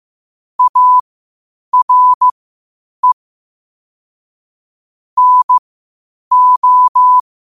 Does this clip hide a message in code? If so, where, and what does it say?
Yes: Morse code from 0.6 to 7.2 s, heard as a keyed tone, about 15 wpm, spaced as for 8 wpm, 1000 Hz. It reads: ARE NO